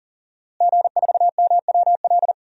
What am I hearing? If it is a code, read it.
Morse code, heard as a keyed tone, about 40 wpm, 700 Hz. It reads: G4MWL
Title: DXCC-100 Call Signs - 5 Characters - Repeated 2x - 40wpm